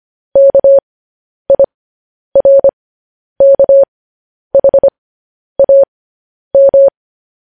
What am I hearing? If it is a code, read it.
Morse code, heard as a keyed tone, about 25 wpm, spaced as for 10 wpm, 550 Hz. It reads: KIRKHAM
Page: news-headlines-in-morse-code-at-10-wpm